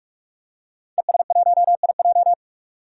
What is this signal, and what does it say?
Morse code, heard as a keyed tone, about 45 wpm, 690 Hz. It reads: ES1IJ
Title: DXCC-100 Call Signs - 5 Characters - 45wpm